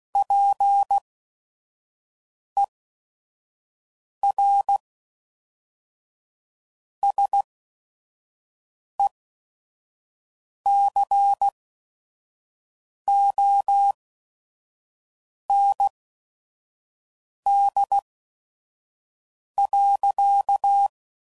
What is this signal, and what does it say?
Morse code, heard as a keyed tone, about 16 wpm, spaced as for 5 wpm, 800 Hz. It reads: PER SECOND.
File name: CW Code Groups @5WPM / Technical Subjects / Reactance